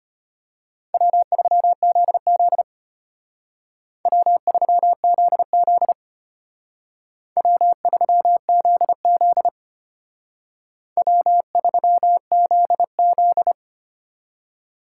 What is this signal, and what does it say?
Morse code, heard as a keyed tone, about 29 wpm, 700 Hz. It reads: W3ZZ W3ZZ W3ZZ W3ZZ
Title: Speed Racing - USA 1x2 Call Signs 25wpm